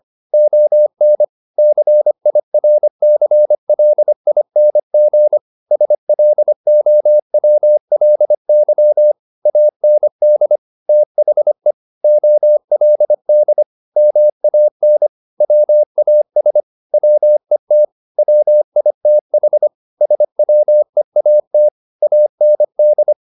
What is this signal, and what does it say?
Morse code, heard as a keyed tone, about 25 wpm, 600 Hz. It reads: ON CIRCLING SLOWLY AND THE OLD MAN WAS WET WITH SWEAT AND